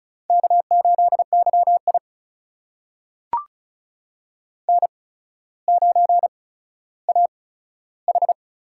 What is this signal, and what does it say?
Morse code, heard as a keyed tone, about 36 wpm, spaced as for 15 wpm, 700 Hz. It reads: K8YI N9AH
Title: USA Call Signs - 4 Characters - 8x Character Spacing - 35wpm